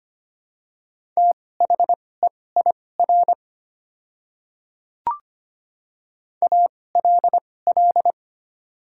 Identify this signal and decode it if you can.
Morse code, heard as a keyed tone, about 25 wpm, spaced as for 18 wpm, 700 Hz. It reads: THEIR ALL
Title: 2x Character Spacing - Top 100 Words - 25wpm